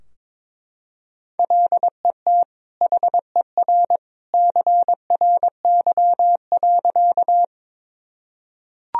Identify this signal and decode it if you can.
Morse code, heard as a keyed tone, about 21 wpm, 700 Hz. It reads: LET HER CRY.
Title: Sentences from Top 1000 Words - Review Part 2 - 22wpm